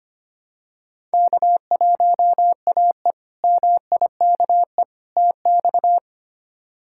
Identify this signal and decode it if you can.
Morse code, heard as a keyed tone, about 25 wpm, 700 Hz. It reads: K1AE MIKE TX
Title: K1USN SST QSO Practice - 25wpm